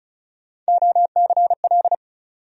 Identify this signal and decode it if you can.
Morse code, heard as a keyed tone, about 36 wpm, 700 Hz. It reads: OCL
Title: USA Call Sign Postfixes - 3 Letters 35wpm